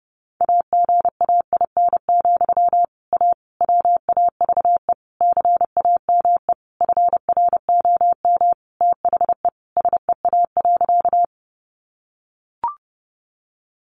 Morse code, heard as a keyed tone, about 28 wpm, 700 Hz. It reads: AGAIN, A WAVE CAME FROM THE SEA.
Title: Sentences from Top 1400 Words 30wpm